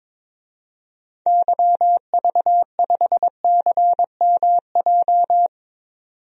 Morse code, heard as a keyed tone, about 22 wpm, 700 Hz. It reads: YV5CMJ
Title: DXCC-100 Call Signs - 6 Characters - 22wpm